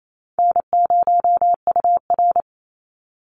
Morse code, heard as a keyed tone, about 28 wpm, 700 Hz. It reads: N0UR
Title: Easy Call Signs Repeated 2x - 28wpm